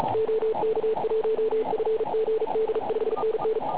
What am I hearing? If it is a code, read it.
Morse code, heard as a keyed tone, about 35 wpm, 440 Hz, with noise. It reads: OK1FGD 5NN
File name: Pile UP CUC204